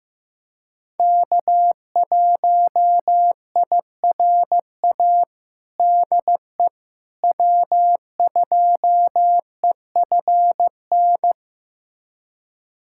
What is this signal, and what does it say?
Morse code, heard as a keyed tone, about 16 wpm, 700 Hz. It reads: K1IRA DE W2EFN
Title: USA Callsign DE Callsign - 5 Characters - 15wpm